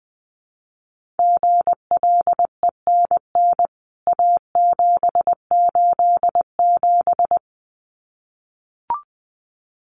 Morse code, heard as a keyed tone, about 20 wpm, 700 Hz. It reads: GLENN A787
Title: CWT Exchange 20wpm